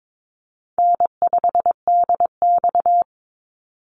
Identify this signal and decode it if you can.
Morse code, heard as a keyed tone, about 22 wpm, 700 Hz. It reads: N5DX
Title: Easy Call Signs - 22wpm